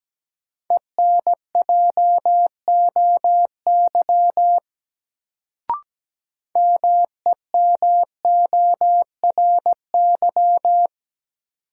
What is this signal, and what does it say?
Morse code, heard as a keyed tone, about 17 wpm, 700 Hz. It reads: ENJOY MEMORY
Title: Sentences from Top 1300 Words - New Word Drill Alt - 17wpm